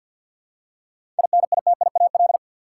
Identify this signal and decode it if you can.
Morse code, heard as a keyed tone, about 49 wpm, 700 Hz. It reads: INITIAL
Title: Seven Letter Words - 50wpm